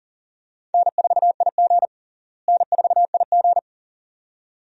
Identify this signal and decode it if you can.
Morse code, heard as a keyed tone, about 40 wpm, 700 Hz. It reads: N4IG N4IG
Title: USA Call Signs - 4 Characters - Repeated 2x - 40wpm